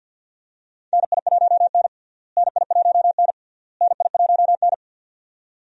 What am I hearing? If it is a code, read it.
Morse code, heard as a keyed tone, about 51 wpm, 700 Hz. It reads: NI1N NI1N NI1N
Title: USA Call Signs - 4 Characters - Repeated 3x - 50wpm